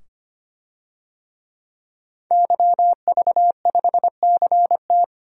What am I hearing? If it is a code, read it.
Morse code, heard as a keyed tone, about 25 wpm, 700 Hz. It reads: YV5CT
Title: DXCC-100 Call Signs - 5 Characters - 25wpm